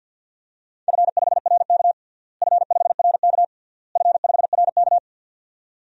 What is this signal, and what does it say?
Morse code, heard as a keyed tone, about 52 wpm, 700 Hz. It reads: F5RK F5RK F5RK
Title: DXCC-100 Call Signs - 4 Characters - Repeated 3x - 50wpm